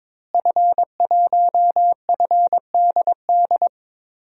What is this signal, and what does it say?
Morse code, heard as a keyed tone, about 22 wpm, 700 Hz. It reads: F1FDD
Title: DXCC-100 Call Signs - 5 Characters - Repeated 2x - 22wpm